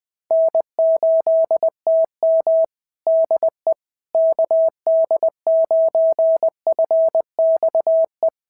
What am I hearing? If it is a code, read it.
Morse code, heard as a keyed tone, about 20 wpm, 660 Hz. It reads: N8TM DE KD9FXE
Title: SOTA QSO Practice 20wpm